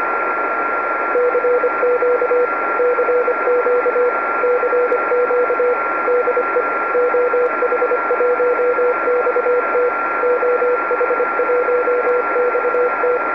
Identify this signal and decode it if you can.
Morse code, heard as a keyed tone, about 26 wpm, 490 Hz, with radio noise. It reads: CQ CQ CQ DE OH1XT OH1XT
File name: oh1xt-2m